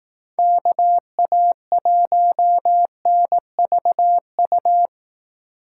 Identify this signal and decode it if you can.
Morse code, heard as a keyed tone, about 18 wpm, 700 Hz. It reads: KA1NVU
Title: Speed Racing - USA 2x3 Call Signs 15wpm